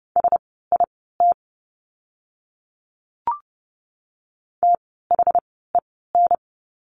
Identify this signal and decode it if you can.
Morse code, heard as a keyed tone, about 30 wpm, spaced as for 17 wpm, 700 Hz. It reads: SIT THEN
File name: course / 30:10wpm / 108 Lesson 18f - Letter m - Words 30wpm